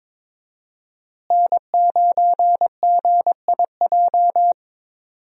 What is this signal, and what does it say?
Morse code, heard as a keyed tone, about 22 wpm, 700 Hz. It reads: N9GIJ
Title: USA Call Signs - 5 Characters - Repeated 2x - 22wpm